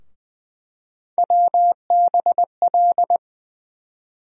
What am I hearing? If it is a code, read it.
Morse code, heard as a keyed tone, about 20 wpm, 700 Hz. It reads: WBL
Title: USA Call Sign Postfixes - 3 Letters 20wpm